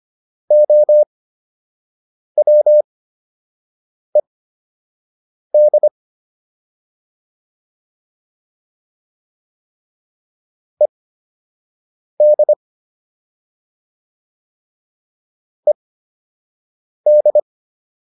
Morse code, heard as a keyed tone, about 25 wpm, spaced as for 6 wpm, 600 Hz. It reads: OWED ED ED